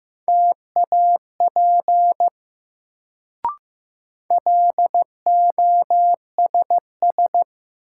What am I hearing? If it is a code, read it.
Morse code, heard as a keyed tone, about 15 wpm, 700 Hz. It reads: TAP LOSS
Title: Sentences from Top 1400 Words - New Word Drill - 15wpm